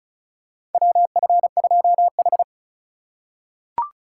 Morse code, heard as a keyed tone, about 36 wpm, 700 Hz. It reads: WF2H
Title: USA Call Signs - 4 Characters - 4x Character Spacing - 35wpm